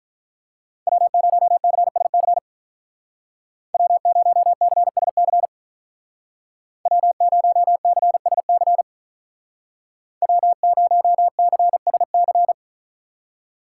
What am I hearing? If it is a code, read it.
Morse code, heard as a keyed tone, about 45 wpm, 700 Hz. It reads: W0CSC W0CSC W0CSC W0CSC